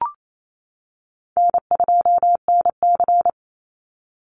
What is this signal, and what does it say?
Morse code, heard as a keyed tone, about 28 wpm, 700 Hz. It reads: N2NC
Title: Call Signs - 28wpm